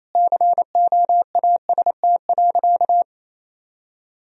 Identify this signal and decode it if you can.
Morse code, heard as a keyed tone, about 28 wpm, 700 Hz. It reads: COAST.
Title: Sentences from Top 1000 Words - Review Part 4 - 28wpm